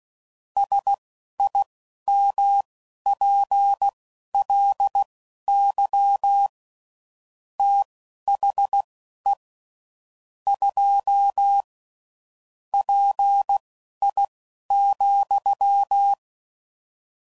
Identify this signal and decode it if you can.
Morse code, heard as a keyed tone, about 16 wpm, spaced as for 11 wpm, 800 Hz. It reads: SIMPLY THE 2 PI,
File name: CW Code Groups @10WPM / Technical Subjects / XL